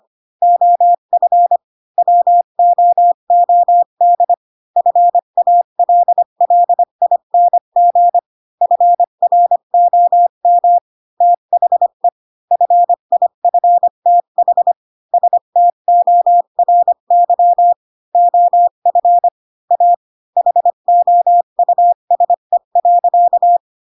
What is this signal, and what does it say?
Morse code, heard as a keyed tone, about 25 wpm, 700 Hz. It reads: OF WOOD FALLING FROM THE FIFTH STORY OF A HOUSE.
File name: Pinocchio / Pinocchio ch7 20 25 ik1hge